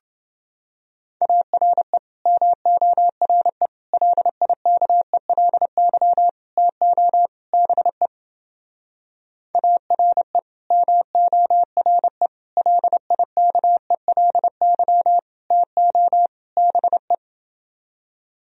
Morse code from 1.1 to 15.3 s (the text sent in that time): ARE MORE LIKELY TO BE ARE MORE LIKELY